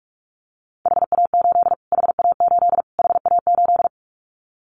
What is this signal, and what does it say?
Morse code, heard as a keyed tone, about 44 wpm, 700 Hz. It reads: HA8 HA8 HA8
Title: Callsign Prefixes for Top 25 DXCC - Repeated 3x - 45wpm